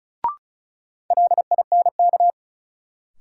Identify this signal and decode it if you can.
Morse code, heard as a keyed tone, about 34 wpm, 700 Hz. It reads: LINK
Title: Sentences from Top 1500 Words - New Word Drill Alt - 35wpm